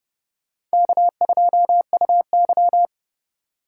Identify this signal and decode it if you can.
Morse code, heard as a keyed tone, about 28 wpm, 700 Hz. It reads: K2UY